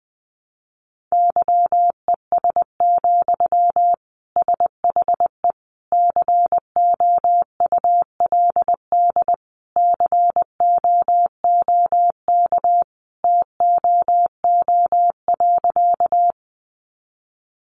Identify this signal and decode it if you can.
Morse code, heard as a keyed tone, about 19 wpm, 700 Hz. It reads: YES, SHE COULD COOK TOO.
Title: Sentences from Top 700 Words 20wpm